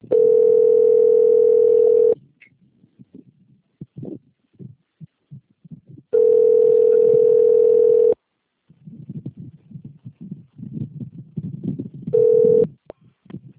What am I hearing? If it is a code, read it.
Ringback tone (North America), 440 + 480 Hz.
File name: out- - - - - .45251